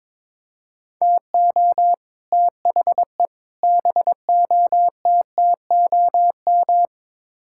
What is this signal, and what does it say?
Morse code, heard as a keyed tone, about 21 wpm, 700 Hz. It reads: TO THE BOTTOM
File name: practice / 22wpm / Build A Phrase - 5-Word Phrases - 22wpm